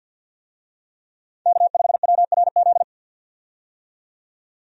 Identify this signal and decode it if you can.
Morse code, heard as a keyed tone, about 47 wpm, 700 Hz. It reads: K5WRZ
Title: USA Call Signs - 5 Characters - 50wpm